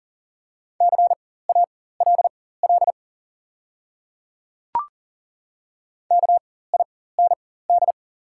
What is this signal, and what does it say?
Morse code, heard as a keyed tone, about 40 wpm, spaced as for 19 wpm, 690 Hz. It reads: CALL KIND
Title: 4x Character Spacing - Top 500 Words - 40wpm